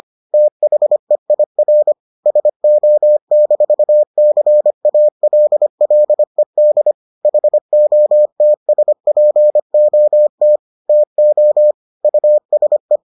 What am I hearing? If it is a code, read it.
Morse code, heard as a keyed tone, about 25 wpm, 600 Hz. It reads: THEIR SO-CALLED HOTSPOT TO USE